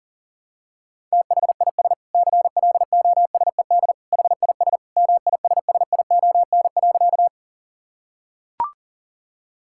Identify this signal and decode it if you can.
Morse code, heard as a keyed tone, about 40 wpm, 700 Hz. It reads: THIS CLOSED HIS MISSION.
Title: Sentences from Top 1600 Words 40wpm